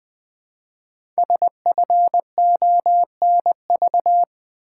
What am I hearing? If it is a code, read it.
Morse code, heard as a keyed tone, about 20 wpm, 700 Hz. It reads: SFONV
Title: Code Groups - 5 Letters - 20wpm